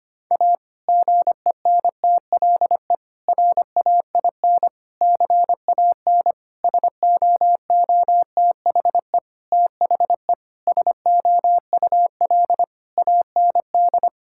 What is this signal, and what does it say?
Morse code, heard as a keyed tone, about 25 wpm, 700 Hz. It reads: A GENTLE RAIN CAN SOOTHE THE SOUL AND
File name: practice / 25wpm / Difficult Sentences with Hints - 25wpm